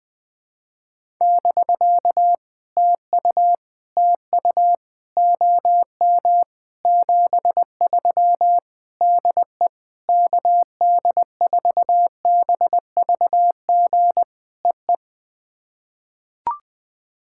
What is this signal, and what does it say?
Morse code, heard as a keyed tone, about 20 wpm, 690 Hz. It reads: <BK> TU TU OM 73 DE KD4BVG EE
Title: POTA QSO Practice - 20wpm